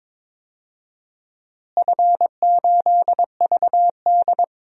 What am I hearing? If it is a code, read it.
Morse code, heard as a keyed tone, about 22 wpm, 700 Hz. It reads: F8VD
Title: DXCC-100 Call Signs - 4 Characters - Repeated 3x - 22wpm